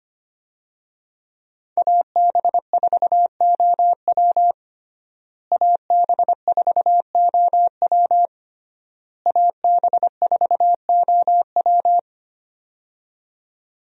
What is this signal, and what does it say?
Morse code, heard as a keyed tone, about 25 wpm, 700 Hz. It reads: AB4OW AB4OW AB4OW
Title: USA Call Signs - 5 Characters - Repeated 3x - 25wpm